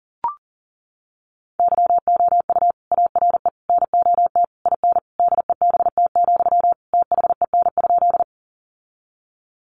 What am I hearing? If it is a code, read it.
Morse code, heard as a keyed tone, about 40 wpm, 700 Hz. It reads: YOU ARE NOT IN DEBT, THEN?
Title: Sentences from Top 1300 Words 40wpm